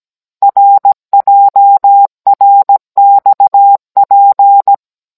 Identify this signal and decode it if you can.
Morse code, heard as a keyed tone, about 17 wpm, 800 Hz. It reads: RJRXP
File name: Cours-NR19-16mm